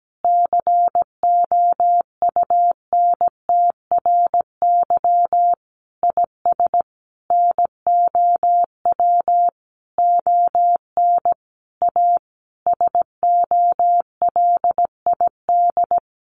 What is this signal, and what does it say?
Morse code, heard as a keyed tone, about 17 wpm, 700 Hz. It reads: COUNTRY IS NOW ON A SOLID
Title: Sentences from Top 1500 Words 17wpm